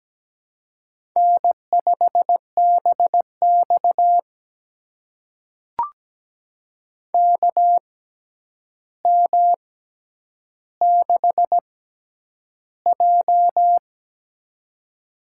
Morse code, heard as a keyed tone, about 17 wpm, spaced as for 9 wpm, 700 Hz. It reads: N5BX KM6J
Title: USA Call Signs - 4 Characters - 6x Character Spacing - 17wpm